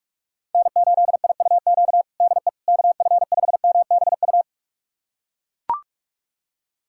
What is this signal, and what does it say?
Morse code, heard as a keyed tone, about 42 wpm, 700 Hz. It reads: N8IUQ DE KF5MBU
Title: USA Callsign DE Callsign 45wpm